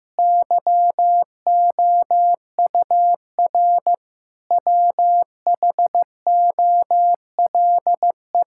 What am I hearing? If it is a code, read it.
Morse code, heard as a keyed tone, about 15 wpm, 700 Hz. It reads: YOUR WHOLE
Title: Sentences from Top 1000 Words - Review Part 5 - 15wpm